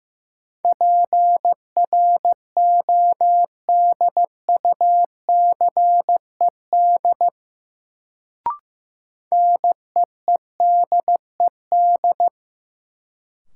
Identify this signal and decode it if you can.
Morse code, heard as a keyed tone, about 15 wpm, 700 Hz. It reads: PRODUCED NEEDED
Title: Sentences from Top 1000 Words - Review Part 6 - New Word Drill - 15wpm